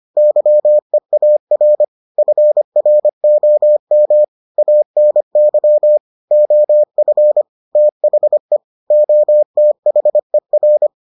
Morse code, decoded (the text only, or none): YEAR FROM ANY OF THE OTHER